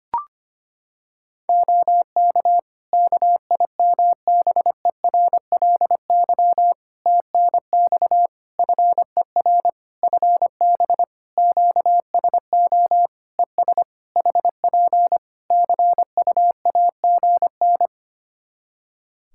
Morse code, heard as a keyed tone, about 25 wpm, 700 Hz. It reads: OK KIMBERLY TNX FER FB QSO ES HP CUAGN
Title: Practice QSO - Exchange 4 - Ending - 25wpm